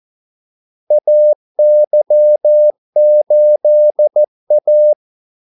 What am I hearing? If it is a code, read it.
Morse code, heard as a keyed tone, about 14 wpm, 600 Hz. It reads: AY8A
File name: Callsigns-14wpm